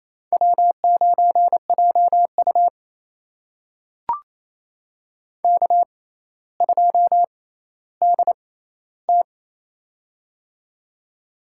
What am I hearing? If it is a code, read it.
Morse code, heard as a keyed tone, about 28 wpm, spaced as for 14 wpm, 700 Hz. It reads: W9JU K2DT
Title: USA Call Signs - 4 Characters - 6x Character Spacing - 28wpm